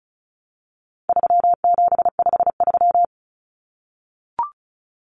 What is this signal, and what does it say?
Morse code, heard as a keyed tone, about 34 wpm, 700 Hz. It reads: 3753
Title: Common Cut Numbers - 4-Digit 35wpm